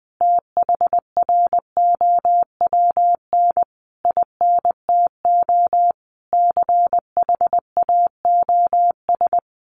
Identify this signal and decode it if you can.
Morse code, heard as a keyed tone, about 20 wpm, 700 Hz. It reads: THROWN INTO CHAOS
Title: A Sound Of Thunder 20wpm